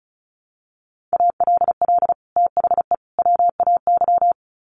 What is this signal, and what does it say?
Morse code, heard as a keyed tone, about 34 wpm, 700 Hz. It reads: ALL THE WAY
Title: Sets of 3 Words 35wpm